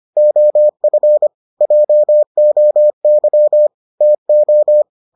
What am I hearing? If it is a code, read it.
Morse code, heard as a keyed tone, about 25 wpm, 600 Hz. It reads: OF JOY TO